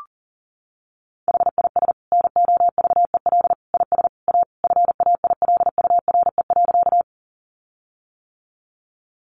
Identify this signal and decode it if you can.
Morse code, heard as a keyed tone, about 40 wpm, 700 Hz. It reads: HIS NOVEL IS A FAILURE.
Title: Sentences from Top 1300 Words 40wpm